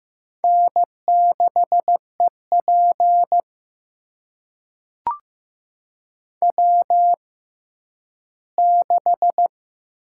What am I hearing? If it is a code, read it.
Morse code, heard as a keyed tone, about 15 wpm, spaced as for 10 wpm, 700 Hz. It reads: N6EP W6
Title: USA Call Signs - 4 Characters - 6x Character Spacing - 15wpm